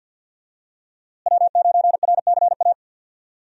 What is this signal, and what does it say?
Morse code, heard as a keyed tone, about 51 wpm, 700 Hz. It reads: W9RCA
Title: USA Call Signs - 5 Characters - Repeated 2x - 50wpm